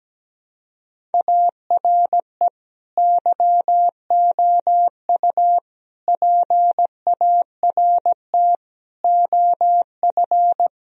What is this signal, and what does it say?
Morse code, heard as a keyed tone, about 17 wpm, 700 Hz. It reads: ARE YOU PART OF
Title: Sentences from Top 1000 Words - Review Part 2 - 17wpm